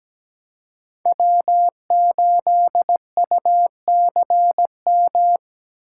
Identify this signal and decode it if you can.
Morse code, heard as a keyed tone, about 17 wpm, 700 Hz. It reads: W8UCM
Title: USA Call Signs - 5 Characters - 2x Character Spacing - 17wpm